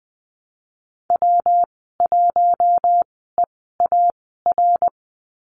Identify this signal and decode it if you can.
Morse code, heard as a keyed tone, about 19 wpm, spaced as for 14 wpm, 700 Hz. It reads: W1EAR
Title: 154 Lesson 25g - Letter p - Call Signs 20wpm